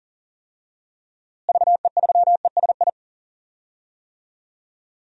Morse code, heard as a keyed tone, about 40 wpm, 700 Hz. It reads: VE3ESI
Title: DXCC-100 Call Signs - 6 Characters - Repeated 2x - 40wpm